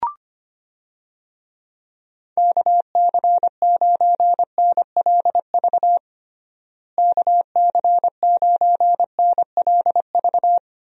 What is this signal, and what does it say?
Morse code, heard as a keyed tone, about 25 wpm, 700 Hz. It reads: KC9NLV KC9NLV
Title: USA Call Signs - 6 Characters - Repeated 3x - 25wpm